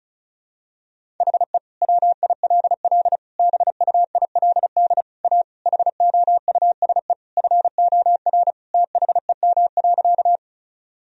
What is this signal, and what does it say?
Morse code, heard as a keyed tone, about 34 wpm, 700 Hz. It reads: HE WILL BUILD A HOUSE FOR THEM.
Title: Sentences from Top 1000 Words - Review Part 1 - 35wpm